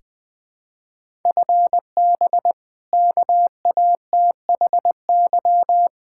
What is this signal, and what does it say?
Morse code, heard as a keyed tone, about 21 wpm, 700 Hz. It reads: FB KATHY
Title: Practice QSO - Exchange 3 - More Info - 20wpm